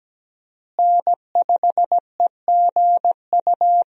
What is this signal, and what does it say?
Morse code, heard as a keyed tone, about 17 wpm, 700 Hz. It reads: N5EGU
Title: USA Call Signs - 5 Characters - Repeated 3x - 17wpm